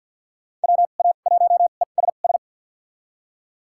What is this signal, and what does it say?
Morse code, heard as a keyed tone, about 51 wpm, spaced as for 36 wpm, 700 Hz. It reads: WA1ESS
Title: USA Call Signs - 2x Character Spacing - 50wpm